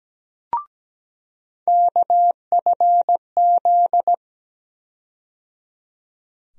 Morse code, heard as a keyed tone, about 17 wpm, 700 Hz. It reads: KFZ
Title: USA Call Sign Postfixes - 3 Letters 17wpm